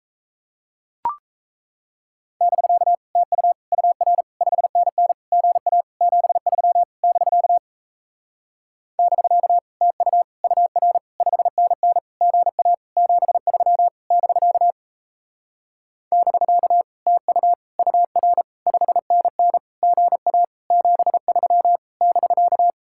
Morse code, heard as a keyed tone, about 38 wpm, 700 Hz. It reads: <BK> TU UR 5NN GA 73 <BK> <BK> TU UR 5NN GA 73 <BK> <BK> TU UR 5NN GA 73 <BK>